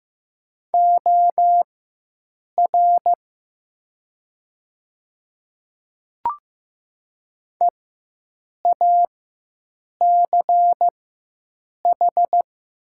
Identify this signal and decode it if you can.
Morse code, heard as a keyed tone, about 15 wpm, spaced as for 7 wpm, 700 Hz. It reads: OR EACH
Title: 4x Character Spacing - Top 100 Words - 15wpm